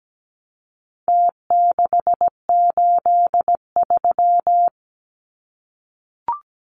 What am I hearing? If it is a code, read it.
Morse code, heard as a keyed tone, about 17 wpm, 700 Hz. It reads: T683
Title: Numbers - Contest Style - 4-Digit 17wpm